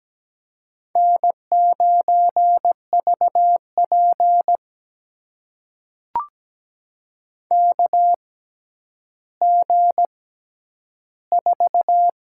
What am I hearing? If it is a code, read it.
Morse code, heard as a keyed tone, about 17 wpm, spaced as for 10 wpm, 700 Hz. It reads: N9VP KG4
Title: USA Call Signs - 6x Character Spacing - 17wpm